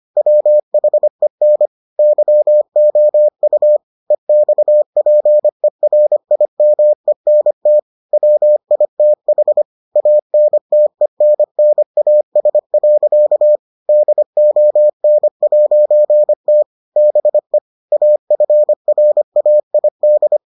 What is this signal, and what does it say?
Morse code, heard as a keyed tone, about 25 wpm, 600 Hz. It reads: WHEN YOU EXPERIMENT WITH ANTENNAS. DON'T BE AFRAID